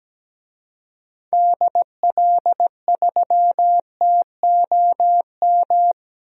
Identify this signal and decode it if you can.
Morse code, heard as a keyed tone, about 17 wpm, 700 Hz. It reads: DL3TOM
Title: DXCC-100 Call Signs - 6 Characters - 17wpm